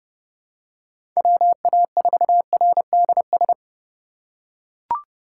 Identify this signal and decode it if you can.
Morse code, heard as a keyed tone, about 30 wpm, 700 Hz. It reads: WA4RDS
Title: USA Call Signs - 6 Characters - 4x Character Spacing - 30wpm